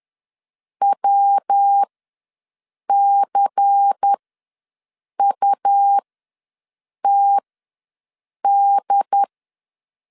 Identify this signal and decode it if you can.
Morse code, heard as a keyed tone, about 10 wpm, spaced as for 6 wpm, 800 Hz. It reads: WCUTD
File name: LEZIONI / 5 LBVC / 5 LBVC ws 10wpm 3733